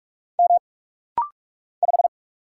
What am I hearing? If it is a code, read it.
Morse code, heard as a keyed tone, about 44 wpm, 700 Hz. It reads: M 5
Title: Single Letter-Number - Rapid-Fire 45wpm